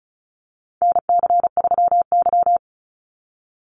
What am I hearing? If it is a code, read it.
Morse code, heard as a keyed tone, about 34 wpm, 700 Hz. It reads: NC3Y
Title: Call Signs - 35wpm